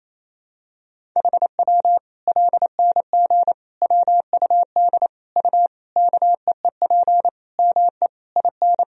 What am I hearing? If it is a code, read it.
Morse code, heard as a keyed tone, about 28 wpm, 700 Hz. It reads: HW LNG WUD U KEEP ME IN
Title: Sentences from Top 500 Words and Abbreviations 28wpm